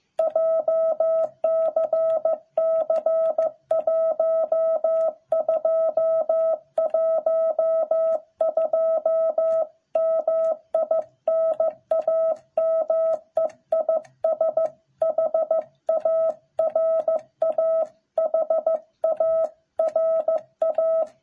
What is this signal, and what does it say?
Morse code, heard as a keyed tone, about 15 wpm, 660 Hz. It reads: JCC 1212 MI NAME IS HARA HARA